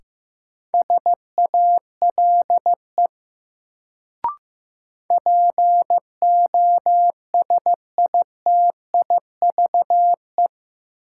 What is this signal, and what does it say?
Morse code, heard as a keyed tone, about 15 wpm, 700 Hz. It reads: SALE POSITIVE
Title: Sentences from Top 1400 Words - New Word Drill Alt - 15wpm